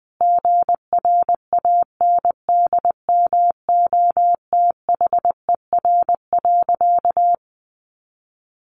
Morse code, heard as a keyed tone, about 20 wpm, 700 Hz. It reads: GRANDMOTHER.
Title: Sentences from Top 1200 Words 20wpm